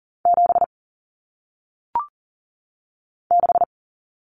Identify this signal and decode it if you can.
Morse code, heard as a keyed tone, about 40 wpm, 700 Hz. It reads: 7 6
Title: Common Cut Numbers - 1-Digit 40wpm